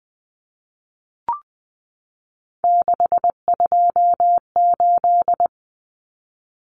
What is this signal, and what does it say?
Morse code, heard as a keyed tone, about 20 wpm, 700 Hz. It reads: 628
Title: Numbers - 3-Digit 20wpm